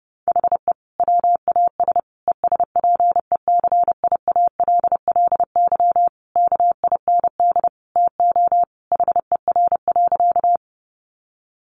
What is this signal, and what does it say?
Morse code, heard as a keyed tone, about 30 wpm, 700 Hz. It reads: HE WAS ESPECIALLY KIND TO HER.
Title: Sentences from Top 700 Words 30wpm